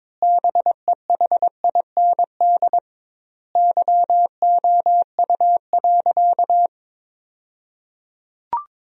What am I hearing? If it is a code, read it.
Morse code, heard as a keyed tone, about 22 wpm, 700 Hz. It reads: BEHIND YOU.
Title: 2x Word-Spacing - Sentences from Top 500 Words - 22wpm